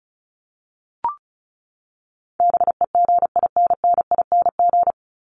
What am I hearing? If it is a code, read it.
Morse code, heard as a keyed tone, about 34 wpm, 700 Hz. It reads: BEGINNING
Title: ING Suffix - Top 100 Words with Base Words 35wpm